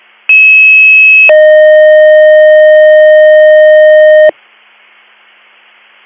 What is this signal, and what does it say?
Two-tone page, heard as a one-second tone, then a three-second tone, about 2600 then 620 Hz. It is heard from 0.3 to 4.3 s.